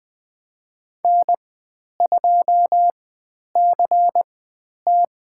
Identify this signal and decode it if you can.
Morse code, heard as a keyed tone, about 20 wpm, spaced as for 10 wpm, 700 Hz. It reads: N2CT
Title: 095 Lesson 16g - Letter c - Call Signs 20wpm